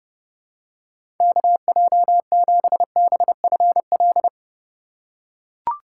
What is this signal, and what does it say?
Morse code, heard as a keyed tone, about 31 wpm, 700 Hz. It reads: KJ7BFL
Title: USA Call Signs - 6 Characters - 4x Character Spacing - 30wpm